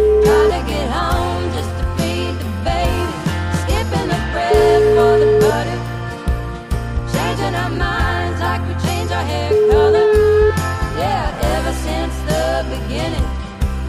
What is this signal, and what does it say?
Ringback tone (Europe, 425 Hz), with noise.